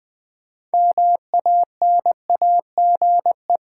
Morse code, heard as a keyed tone, about 20 wpm, 700 Hz. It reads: MANAGE
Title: Build A Word - 2 Letters At A Time - 6-Letter Words - 20wpm